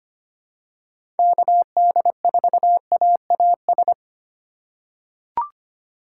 Morse code, heard as a keyed tone, about 25 wpm, 700 Hz. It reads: KD4AAS
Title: USA Call Signs - 6 Characters - 2x Character Spacing - 25wpm